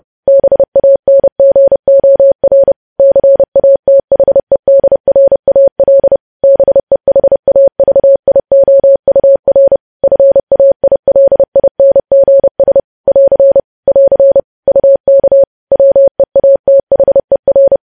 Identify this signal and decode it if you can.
Morse code, heard as a keyed tone, about 30 wpm, 550 Hz. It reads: BANGOR CATHEDRAL BEHAVIOUR FAILINGS <AR> <AR> UK WEATHER
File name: podcast / News headlines-1x-30x30